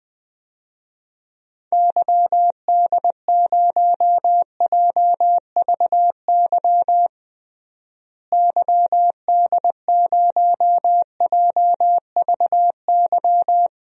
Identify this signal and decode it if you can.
Morse code, heard as a keyed tone, about 20 wpm, 700 Hz. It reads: YD0JVY YD0JVY
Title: DXCC-100 Call Signs - 6 Characters - Repeated 2x - 20wpm